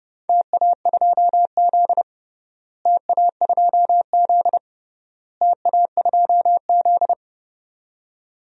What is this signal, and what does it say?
Morse code, heard as a keyed tone, about 30 wpm, 700 Hz. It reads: TA2Z TA2Z TA2Z
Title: DXCC-100 Call Signs - 4 Characters - Repeated 3x - 30wpm